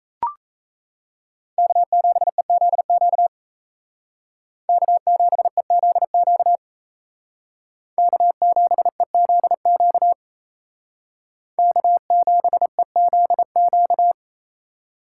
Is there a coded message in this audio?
Yes: Morse code from 1.6 to 14.1 s, heard as a keyed tone, about 35 wpm, 700 Hz. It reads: K7EZQ K7EZQ K7EZQ K7EZQ